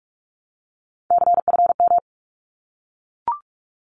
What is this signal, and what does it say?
Morse code, heard as a keyed tone, about 44 wpm, 700 Hz. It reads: CFM
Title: Abbreviations Repeated 3x - 45wpm